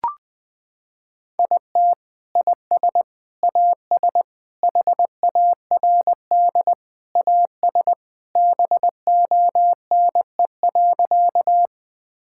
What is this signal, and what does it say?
Morse code, heard as a keyed tone, about 21 wpm, 700 Hz. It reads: IT IS AS HARD AS BONE.
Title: Sentences from Top 1600 Words 20wpm